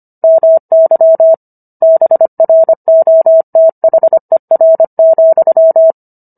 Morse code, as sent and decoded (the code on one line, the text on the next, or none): -- -.-- / -... .-. --- - .... .-. --..--
MY BROTHER,